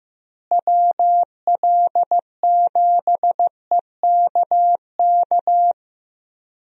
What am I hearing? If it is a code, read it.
Morse code, heard as a keyed tone, about 15 wpm, 700 Hz. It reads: WL7EKK
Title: USA Call Signs - 6 Characters - Repeated 3x - 15wpm